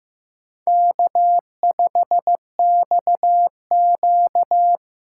Morse code, heard as a keyed tone, about 15 wpm, 700 Hz. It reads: K5XQ
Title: K1USN SST QSO Practice - 15wpm